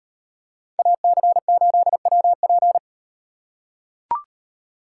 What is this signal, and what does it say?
Morse code, heard as a keyed tone, about 37 wpm, 700 Hz. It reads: AC8WP